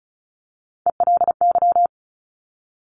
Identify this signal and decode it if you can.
Morse code, heard as a keyed tone, about 36 wpm, 700 Hz. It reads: ELY
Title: Top Three Letter Combinations in Words 35wpm